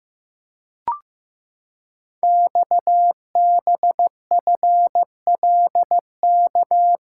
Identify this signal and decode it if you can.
Morse code, heard as a keyed tone, about 15 wpm, 700 Hz. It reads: XBFLK
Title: Code Groups - 5 Letters - 15wpm